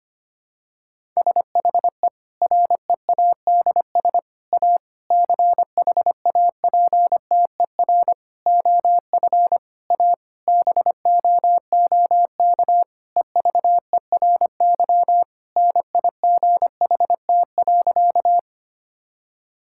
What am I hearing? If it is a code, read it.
Morse code, heard as a keyed tone, about 25 wpm, 700 Hz. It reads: SHE READS A CHAPTER OF A BOOK EVERY NIGHT.
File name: practice / 25wpm / Difficult Sentences with Hints - 25wpm